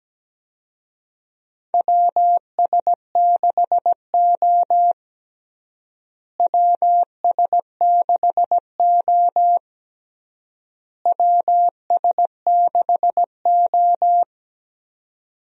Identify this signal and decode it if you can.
Morse code, heard as a keyed tone, about 17 wpm, 700 Hz. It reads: WS6O WS6O WS6O
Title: USA Call Signs - 4 Characters - Repeated 3x - 17wpm